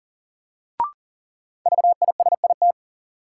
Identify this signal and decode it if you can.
Morse code, heard as a keyed tone, about 40 wpm, 700 Hz. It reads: VISIT
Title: Sentences from Top 900 Words - New Word Drill - 40wpm